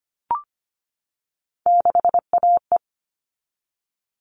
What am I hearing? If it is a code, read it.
Morse code, heard as a keyed tone, about 25 wpm, 700 Hz. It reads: 6AE
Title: All Cut Numbers - 3-Digit 25wpm